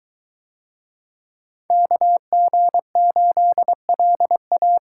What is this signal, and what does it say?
Morse code, heard as a keyed tone, about 21 wpm, 700 Hz. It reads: KG8LA